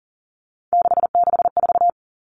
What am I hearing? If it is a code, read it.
Morse code, heard as a keyed tone, about 40 wpm, 700 Hz. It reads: 664
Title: Common Cut Numbers - 3-Digit 40wpm